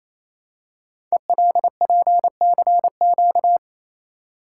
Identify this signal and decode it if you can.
Morse code, heard as a keyed tone, about 29 wpm, 700 Hz. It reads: ELPCQ
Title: Code Groups - 5 Letters - 28wpm